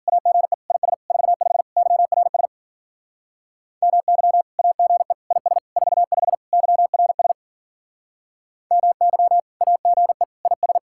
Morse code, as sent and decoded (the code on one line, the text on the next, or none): .- --. / .. ... / ....- ..... / -.-- .-. ... / -- -.-- / .- --. / .. ... / ....- ..... / -.-- .-. ... / -- -.-- / .- --. / .. ...
AGE IS 45 YRS MY AGE IS 45 YRS MY AGE IS